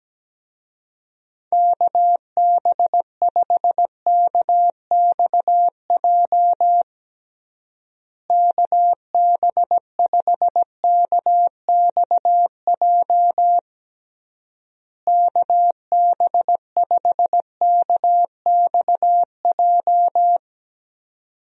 Morse code, heard as a keyed tone, about 17 wpm, 700 Hz. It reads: KB5KXJ KB5KXJ KB5KXJ
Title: USA Call Signs - 6 Characters - Repeated 3x - 17wpm